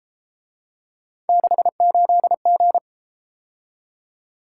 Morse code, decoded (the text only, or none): 68G